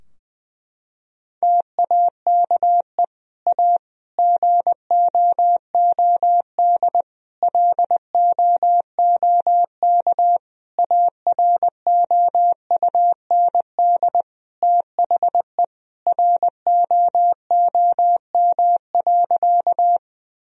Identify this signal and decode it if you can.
Morse code, heard as a keyed tone, about 20 wpm, 700 Hz. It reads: TAKE A GOOD LOOK AROUND THE ROOM.
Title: Sentences from Top 1000 Words - Review Part 3 - 20wpm